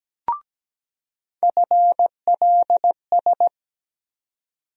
Morse code, heard as a keyed tone, about 17 wpm, 700 Hz. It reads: FLS
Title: USA Call Sign Postfixes - 3 Letters 17wpm